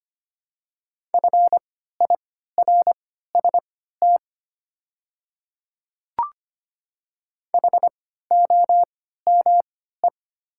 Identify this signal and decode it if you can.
Morse code, heard as a keyed tone, about 25 wpm, spaced as for 14 wpm, 690 Hz. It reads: FIRST HOME
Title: 3x Character Spacing - Top 200 Words - 25wpm